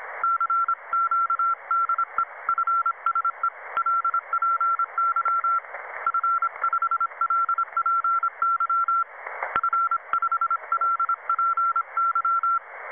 Morse code, heard as a keyed tone, about 26 wpm, 1340 Hz. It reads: CQ DE FSELPY F5LPY F5LPY
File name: Websdr Recording 2014 07 21t07 22 13z 7011 4khz